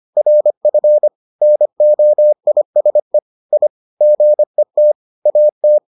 Morse code, heard as a keyed tone, about 25 wpm, 600 Hz. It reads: RF NOISE I GET AT